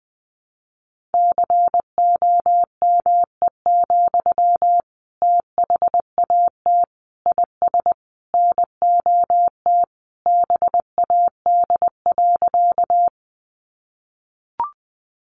Morse code, heard as a keyed tone, about 19 wpm, 700 Hz. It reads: COME, THAT IS NOT BAD.
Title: Sentences from Top 700 Words 20wpm